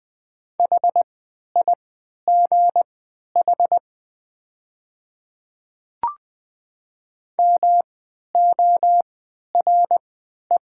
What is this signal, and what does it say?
Morse code, heard as a keyed tone, about 20 wpm, spaced as for 11 wpm, 700 Hz. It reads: HIGH MORE
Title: 3x Character Spacing - Top 300 Words - 20wpm